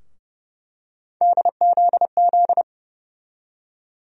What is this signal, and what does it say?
Morse code, heard as a keyed tone, about 30 wpm, 700 Hz. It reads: DZZ
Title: USA Call Sign Postfixes - 3 Letters 30wpm